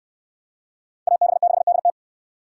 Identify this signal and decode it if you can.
Morse code, heard as a keyed tone, about 69 wpm, 700 Hz. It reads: AB6DT